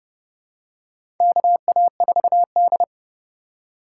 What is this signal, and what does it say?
Morse code, heard as a keyed tone, about 31 wpm, 700 Hz. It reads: KA4D
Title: USA Call Signs - 4 Characters - 2x Character Spacing - 30wpm